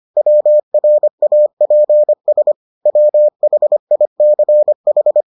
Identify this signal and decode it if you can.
Morse code, heard as a keyed tone, about 25 wpm, 600 Hz. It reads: WRAPS WHICH